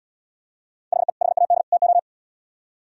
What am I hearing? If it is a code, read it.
Morse code, heard as a keyed tone, about 76 wpm, 700 Hz. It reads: HE HAD TO